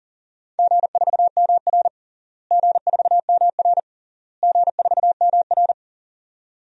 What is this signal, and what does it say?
Morse code, heard as a keyed tone, about 40 wpm, 700 Hz. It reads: G4MR G4MR G4MR
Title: DXCC-100 Call Signs - 4 Characters - Repeated 3x - 40wpm